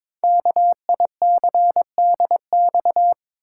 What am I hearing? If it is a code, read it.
Morse code, heard as a keyed tone, about 22 wpm, 700 Hz. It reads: KICDX
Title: Code Groups - 5 Letter:Numbers - 22wpm